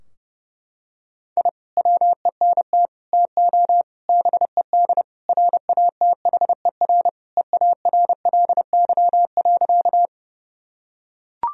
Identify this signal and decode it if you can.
Morse code, heard as a keyed tone, about 30 wpm, 700 Hz. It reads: I WENT TO BED RATHER EARLY.
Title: Sentences from Top 1000 Words - Review Part 4 - 30wpm